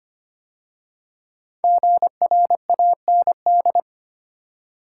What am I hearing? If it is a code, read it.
Morse code, heard as a keyed tone, about 25 wpm, 700 Hz. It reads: GRAND
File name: 5 Letter Words - First 3 Letters Given - 25wpm